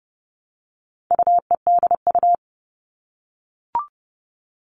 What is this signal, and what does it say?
Morse code, heard as a keyed tone, about 30 wpm, 700 Hz. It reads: UEDU
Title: All Cut Numbers - 4-Digit 30wpm